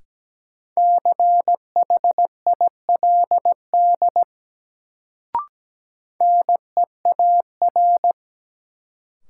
Morse code, heard as a keyed tone, about 17 wpm, 700 Hz. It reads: CHILD NEAR
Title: Sets of 5 Words Encore - New Word Drill - 17wpm